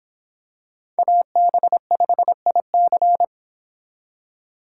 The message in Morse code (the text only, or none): AB5IC